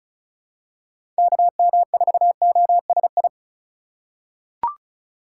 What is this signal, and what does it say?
Morse code, heard as a keyed tone, about 34 wpm, 700 Hz. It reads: KM4OSI
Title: USA Call Signs - 6 Characters - 4x Character Spacing - 35wpm